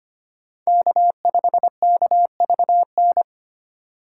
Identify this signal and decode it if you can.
Morse code, heard as a keyed tone, about 32 wpm, 700 Hz. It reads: K5KVN